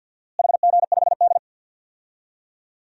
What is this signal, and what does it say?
Morse code, heard as a keyed tone, about 51 wpm, 700 Hz. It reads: HG5D
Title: DXCC-100 Call Signs - 4 Characters - Repeated 3x - 50wpm